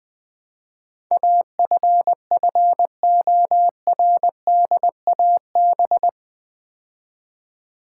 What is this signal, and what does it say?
Morse code, heard as a keyed tone, about 20 wpm, 700 Hz. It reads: AFFORDAB
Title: Build A Word - 4 Letters At A Time - 10-Letter Words - 20wpm